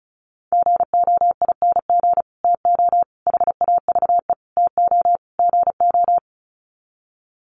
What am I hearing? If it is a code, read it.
Morse code, heard as a keyed tone, about 34 wpm, 700 Hz. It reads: GOING TO HAVE TO GO
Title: Sets of 5 Words Encore 35wpm